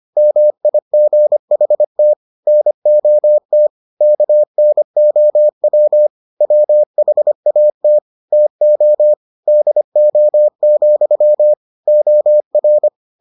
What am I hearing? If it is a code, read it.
Morse code, heard as a keyed tone, about 25 wpm, 600 Hz. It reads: MIGHT NOT KNOW WHAT TO DO, OR